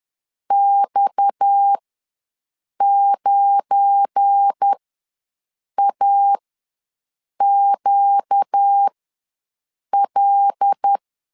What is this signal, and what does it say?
Morse code, heard as a keyed tone, about 10 wpm, spaced as for 6 wpm, 800 Hz. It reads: X9AQL
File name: LEZIONI / 7 GF9Z / 7 GF9 ws 10wpm 4034